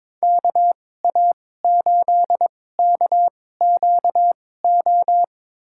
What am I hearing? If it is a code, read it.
Morse code, heard as a keyed tone, about 22 wpm, spaced as for 16 wpm, 700 Hz. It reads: KA8KQO
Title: USA Call Signs - 2x Character Spacing - 22wpm